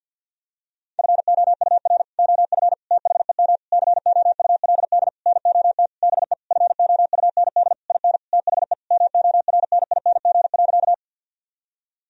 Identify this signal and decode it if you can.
Morse code, heard as a keyed tone, about 51 wpm, 700 Hz. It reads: FOUR OF THEM COULD NOT BE FOUND IN THE MORNING.
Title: Sentences from Top 1000 Words - Review Part 1 - 50wpm